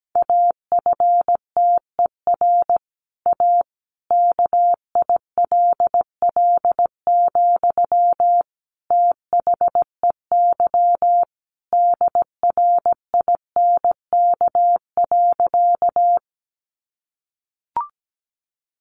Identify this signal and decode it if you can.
Morse code, heard as a keyed tone, about 17 wpm, 700 Hz. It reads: AFTER A KILL, THEY DRINK.
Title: Sentences from Top 1500 Words 17wpm